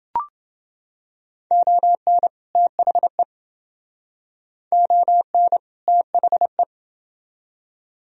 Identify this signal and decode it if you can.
Morse code, heard as a keyed tone, about 28 wpm, 700 Hz. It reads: ON THE ON THE